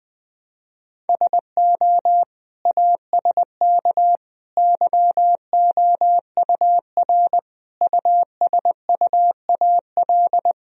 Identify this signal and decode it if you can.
Morse code, heard as a keyed tone, about 20 wpm, 700 Hz. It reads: SO ASK YOUR USUAL
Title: Sentences from Top 1000 Words - Review Part 2 - 20wpm